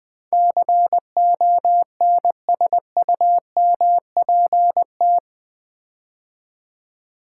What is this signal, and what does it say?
Morse code, heard as a keyed tone, about 20 wpm, 700 Hz. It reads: CONSUMPT
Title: Build A Word - 4 Letters At A Time - 11-Letter Words - 20wpm